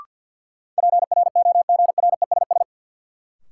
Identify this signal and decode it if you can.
Morse code, heard as a keyed tone, about 49 wpm, 700 Hz. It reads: PROGRESS
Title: Sentences from Top 1400 Words - New Word Drill - 50wpm